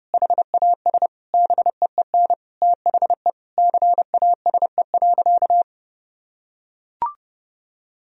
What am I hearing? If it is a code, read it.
Morse code, heard as a keyed tone, about 30 wpm, 700 Hz. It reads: HAS BEEN THE CASE.
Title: Sentences from Top 1000 Words - Review Part 3 - 30wpm